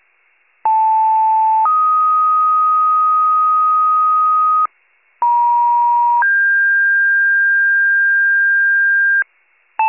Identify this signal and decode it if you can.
Two-tone page, heard as a one-second tone, then a three-second tone, about 880 then 1250 Hz, about 950 then 1600 Hz, with noise. It is heard from 0.6 to 9.2 s.